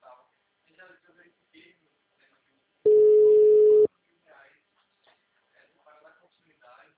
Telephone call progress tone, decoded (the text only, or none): ringback tone